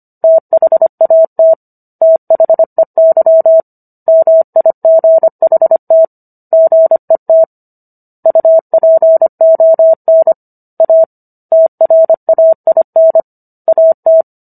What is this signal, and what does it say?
Morse code, heard as a keyed tone, about 25 wpm, 650 Hz. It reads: THAT THEY MIGHT GET UPON A TRAIN AT